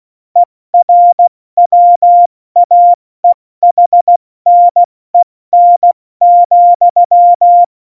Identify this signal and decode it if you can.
Morse code, heard as a keyed tone, about 16 wpm, 700 Hz. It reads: ERWAEHNEN,
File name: Morsekurs / Witze / Witz 18. Februar 2020, 70 F80